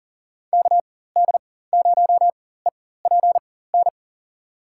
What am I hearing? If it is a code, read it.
Morse code, heard as a keyed tone, about 40 wpm, spaced as for 19 wpm, 700 Hz. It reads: KD0EPN
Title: USA Call Signs - 6 Characters - 4x Character Spacing - 40wpm